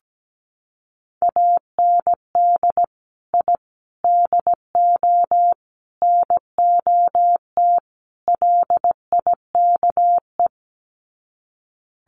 Morse code, heard as a keyed tone, about 17 wpm, 700 Hz. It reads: AND I DO NOT LIKE
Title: Sets of 5 Words Encore2 17wpm